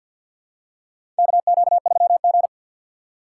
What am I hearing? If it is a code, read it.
Morse code, heard as a keyed tone, about 52 wpm, 700 Hz. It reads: KQ3G
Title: USA Call Signs - 4x Character Spacing - 50wpm